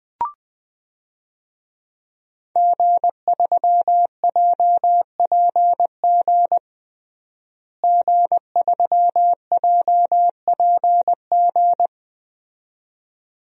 Morse code, heard as a keyed tone, about 20 wpm, 700 Hz. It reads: G3JPG G3JPG
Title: DXCC-100 Call Signs - 5 Characters - Repeated 2x - 20wpm